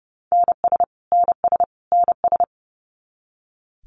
Morse code, heard as a keyed tone, about 30 wpm, 700 Hz. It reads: NS NS NS
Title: Canadian Provinces and Territories - Repeated 3x - 30wpm